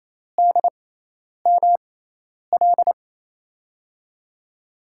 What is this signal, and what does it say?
Morse code, heard as a keyed tone, about 28 wpm, spaced as for 10 wpm, 700 Hz. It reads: DML
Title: USA Call Signs - 5 Characters - 6x Character Spacing - 28wpm